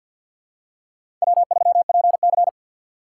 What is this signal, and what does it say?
Morse code, heard as a keyed tone, about 53 wpm, 700 Hz. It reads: W3PC
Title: USA Call Signs - 2x Character Spacing - 50wpm